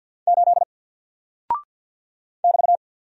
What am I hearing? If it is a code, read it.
Morse code, heard as a keyed tone, about 48 wpm, 700 Hz. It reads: 8 <BT>
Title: Single Character (Letter:Number:Punctuation:Prosign) Alt 50wpm